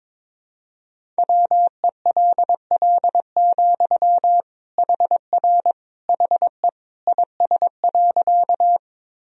Morse code, heard as a keyed tone, about 22 wpm, 700 Hz. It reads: WELL, HR HE IS.
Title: Sentences from Top 200 Words and Abbreviations 22wpm